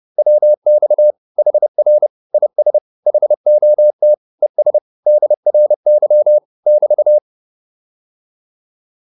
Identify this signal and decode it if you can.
Morse code, heard as a keyed tone, about 30 wpm, 600 Hz. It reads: WX HR IS HOT ES DRY <BT>